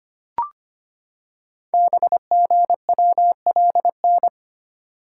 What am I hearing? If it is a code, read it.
Morse code, heard as a keyed tone, about 25 wpm, 700 Hz. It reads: BGWLN
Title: Code Groups - 5 Letters - 25wpm